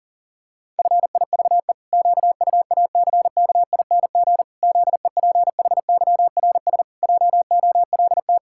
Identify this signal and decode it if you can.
Morse code, heard as a keyed tone, about 40 wpm, 700 Hz. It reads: FIVE QUACKING ZEPHYRS JOLT
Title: Pangram Challenge v2 - 40wpm